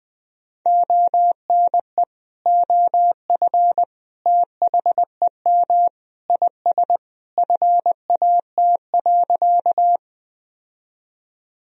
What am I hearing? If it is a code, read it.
Morse code, heard as a keyed tone, about 20 wpm, 700 Hz. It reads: ONE OF THEM IS FAT.
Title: Sentences from Top 1000 Words - Review Part 4 - 20wpm